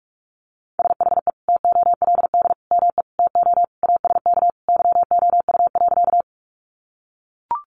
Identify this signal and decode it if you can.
Morse code, heard as a keyed tone, about 44 wpm, 700 Hz. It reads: SHE TOLD ME TO ASK YOU.
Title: Sentences from Top 400 Words 45wpm